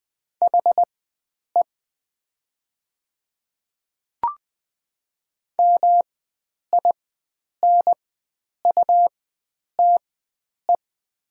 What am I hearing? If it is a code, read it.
Morse code, heard as a keyed tone, about 20 wpm, spaced as for 9 wpm, 690 Hz. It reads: HE MINUTE
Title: 4x Character Spacing - Top 500 Words - 20wpm